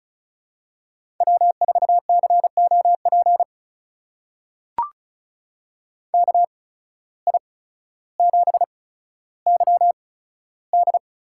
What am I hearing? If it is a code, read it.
Morse code, heard as a keyed tone, about 36 wpm, spaced as for 15 wpm, 700 Hz. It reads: W4COP KI7YD
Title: USA Call Signs - 8x Character Spacing - 35wpm